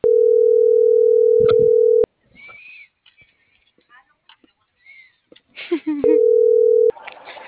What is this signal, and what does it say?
Ringback tone (North America), 440 + 480 Hz.